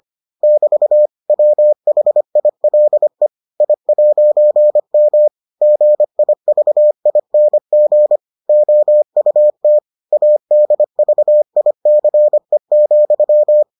Morse code, heard as a keyed tone, about 25 wpm, 600 Hz. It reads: <BT> WHILE I'M GIVING OUT ADVICE,